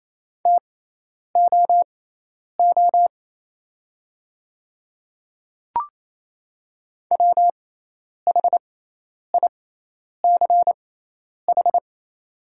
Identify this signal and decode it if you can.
Morse code, heard as a keyed tone, about 27 wpm, spaced as for 10 wpm, 700 Hz. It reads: TOO WHICH
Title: 6x Character Spacing - Top 200 Words - 28wpm